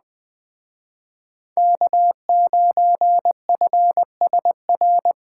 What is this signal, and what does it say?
Morse code, heard as a keyed tone, about 20 wpm, 700 Hz. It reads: K9FSR
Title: Speed Racing - USA 1x3 Call Signs 17wpm